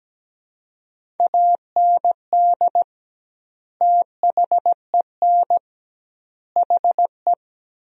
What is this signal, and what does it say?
Morse code, heard as a keyed tone, about 17 wpm, 700 Hz. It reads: AND THEN HE
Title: 2x Word-Spacing - Sentences from Top 400 Words - 17wpm